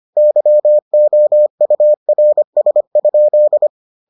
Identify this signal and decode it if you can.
Morse code, heard as a keyed tone, about 25 wpm, 600 Hz. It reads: YOURS?